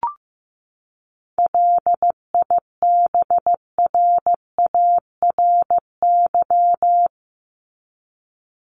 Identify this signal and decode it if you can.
Morse code, heard as a keyed tone, about 15 wpm, 700 Hz. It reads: LIBRARY
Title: Top 501-750 Words 15wpm